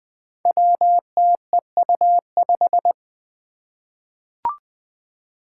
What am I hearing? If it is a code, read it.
Morse code, heard as a keyed tone, about 20 wpm, 700 Hz. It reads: WTEU5
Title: Five Challenging Letter-Number Combinations 20wpm